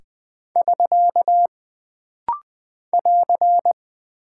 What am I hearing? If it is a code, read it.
Morse code, heard as a keyed tone, about 20 wpm, 700 Hz. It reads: <SK> <AR>
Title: Single Punctuation-Prosign - Rapid-Fire - 20wpm